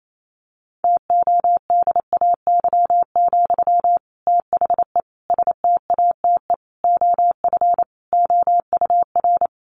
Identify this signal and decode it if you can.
Morse code, heard as a keyed tone, about 28 wpm, 700 Hz. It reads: TODAY, THE STATE OF OUR
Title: Sentences from Top 1400 Words 28wpm